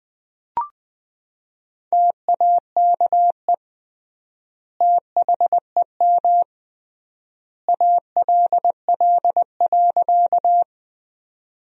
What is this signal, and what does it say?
Morse code, heard as a keyed tone, about 20 wpm, 700 Hz. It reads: TAKE THEM ALL.
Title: 3x Word-Spacing - Sentences from Top 200 Words - 20wpm